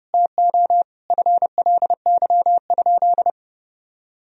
Morse code, heard as a keyed tone, about 30 wpm, 700 Hz. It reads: TO FLY?
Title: Sentences from Top 1000 Words - Review Part 2 - 30wpm